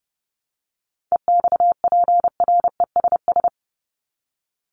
Morse code, heard as a keyed tone, about 30 wpm, 700 Hz. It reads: EXPRESS
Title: ED Suffix - Top 100 Words with Base Words 30wpm